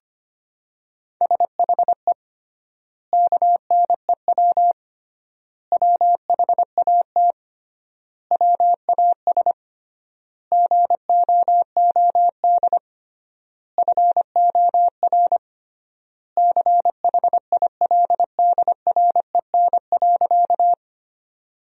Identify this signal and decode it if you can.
Morse code, heard as a keyed tone, about 25 wpm, 700 Hz. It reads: SHE KNEW WHAT WAS GOOD FOR CHILDREN.
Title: 3x Word-Spacing - Sentences from Top 400 Words - 25wpm